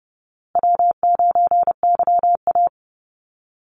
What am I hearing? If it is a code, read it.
Morse code, heard as a keyed tone, about 30 wpm, 700 Hz. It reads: W9YA
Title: Call Signs Repeated 2x - 30wpm